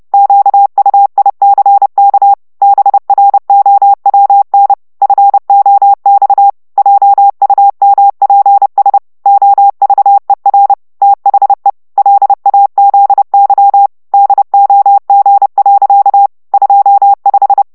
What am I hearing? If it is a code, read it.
Morse code, heard as a keyed tone, about 30 wpm, 800 Hz. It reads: QUICK BROWN FOX JUMPS OVER THE LAZY DOG. 25
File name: szamolo / CW 25 baud